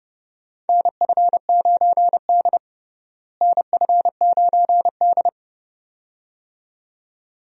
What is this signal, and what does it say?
Morse code, heard as a keyed tone, about 32 wpm, 700 Hz. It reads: NF9D NF9D
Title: USA Call Signs - 4 Characters - Repeated 2x - 30wpm